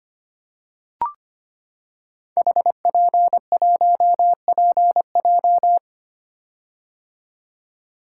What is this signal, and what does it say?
Morse code, heard as a keyed tone, about 25 wpm, 700 Hz. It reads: HP1PJ
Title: Code Groups - 5 Letter:Numbers - 25wpm